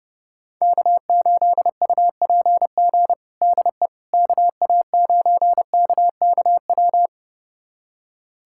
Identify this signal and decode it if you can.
Morse code, heard as a keyed tone, about 32 wpm, 700 Hz. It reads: K8UPG DE KA9KKW
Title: USA Callsign DE Callsign 30wpm